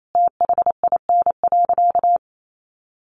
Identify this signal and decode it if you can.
Morse code, heard as a keyed tone, about 28 wpm, 700 Hz. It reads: THIN.
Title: Sentences from Top 1200 Words 28wpm